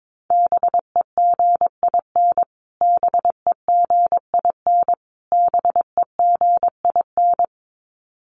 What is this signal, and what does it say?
Morse code, heard as a keyed tone, about 22 wpm, 700 Hz. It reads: BEGIN BEGIN BEGIN
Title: Top 300 Words Repeated 22wpm